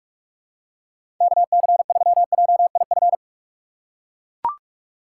Morse code, heard as a keyed tone, about 44 wpm, 700 Hz. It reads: KC3JIF
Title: USA Call Signs - 6 Characters - 8x Character Spacing - 45wpm